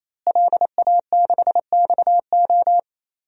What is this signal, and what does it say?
Morse code, heard as a keyed tone, about 28 wpm, 700 Hz. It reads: LA6XO
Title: DXCC-100 Call Signs - 5 Characters - Repeated 3x - 28wpm